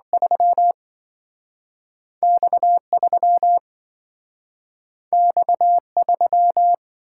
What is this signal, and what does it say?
Morse code, heard as a keyed tone, about 27 wpm, 700 Hz. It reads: 3 X3 X3